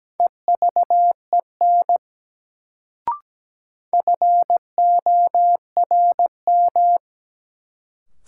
Morse code, heard as a keyed tone, about 17 wpm, 700 Hz. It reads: EVEN FORM
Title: Sets of 4 Words - New Word Drill - 17wpm